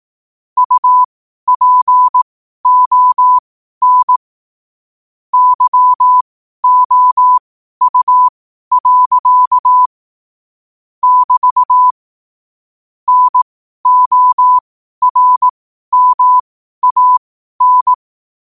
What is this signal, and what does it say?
Morse code, heard as a keyed tone, about 18 wpm, spaced as for 12 wpm, 1000 Hz. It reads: UPON YOU. <BT> NORMAN
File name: qotdfiles / 12wpm